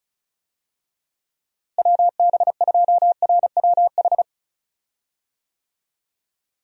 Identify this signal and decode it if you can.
Morse code, heard as a keyed tone, about 36 wpm, 700 Hz. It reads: WB2RWH
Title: USA Call Signs - 6 Characters - Repeated 3x - 35wpm